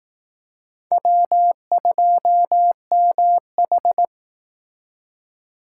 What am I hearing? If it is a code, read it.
Morse code, heard as a keyed tone, about 18 wpm, 700 Hz. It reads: W2MH
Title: Speed Racing - USA 1x2 Call Signs 15wpm